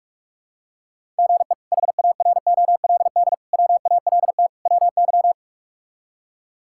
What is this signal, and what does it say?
Morse code, heard as a keyed tone, about 43 wpm, 700 Hz. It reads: GE HAROLD WALT WY
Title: K1USN SST QSO Practice - 45wpm